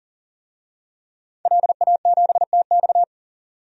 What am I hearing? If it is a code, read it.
Morse code, heard as a keyed tone, about 40 wpm, 700 Hz. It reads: LA7TX
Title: DXCC-100 Call Signs - 5 Characters - 40wpm